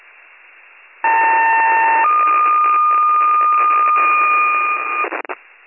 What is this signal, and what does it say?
Two-tone page, heard as a one-second tone, then a three-second tone, about 900 then 1220 Hz, with noise. It is heard from 1.0 to 5.0 s.